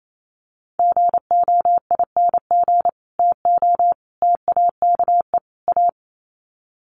Morse code, heard as a keyed tone, about 28 wpm, 700 Hz. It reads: GOING TO TAKE A
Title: Sets of 4 Words 28wpm